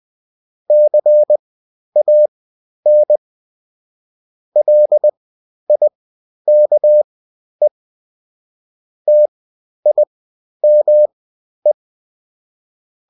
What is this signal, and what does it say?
Morse code, heard as a keyed tone, about 20 wpm, spaced as for 11 wpm, 600 Hz. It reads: CAN LIKE TIME
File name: Top-100-Words-10